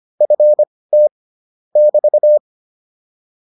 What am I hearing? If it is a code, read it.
Morse code, heard as a keyed tone, about 25 wpm, spaced as for 18 wpm, 600 Hz. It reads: FT <BT>